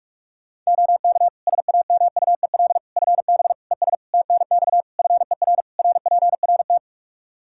Morse code, heard as a keyed tone, about 46 wpm, 700 Hz. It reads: OK SAMUEL FB ES TNX FER RPRT
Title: Practice QSO - Exchange 2 - Info - 45wpm